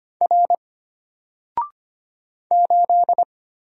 Morse code, heard as a keyed tone, about 25 wpm, 700 Hz. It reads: R 8
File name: Single-Letter-Number-25wpm